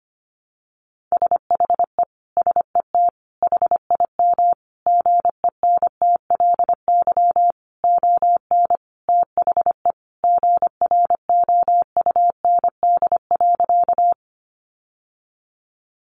Morse code, heard as a keyed tone, about 25 wpm, 700 Hz. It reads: SHE SET HIM GENTLY ON THE GROUND.
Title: Sentences from Top 1500 Words 25wpm